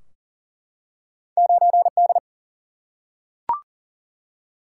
Morse code, heard as a keyed tone, about 41 wpm, 700 Hz. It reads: 9D
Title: Single-Letter Single-Number 40wpm